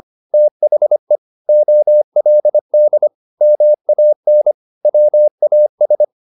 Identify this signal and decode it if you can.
Morse code, heard as a keyed tone, about 25 wpm, 600 Hz. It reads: THE OLD MAN WAS
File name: the old man and the sea part 1 25wpm